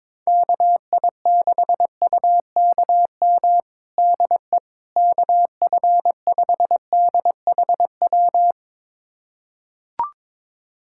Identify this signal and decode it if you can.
Morse code, heard as a keyed tone, about 22 wpm, 700 Hz. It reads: KI6UKM DE KF5DHW
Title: USA Callsign DE Callsign - 6 Characters - 22wpm